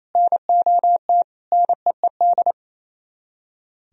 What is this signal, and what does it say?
Morse code, heard as a keyed tone, about 37 wpm, 700 Hz. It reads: NOT NEED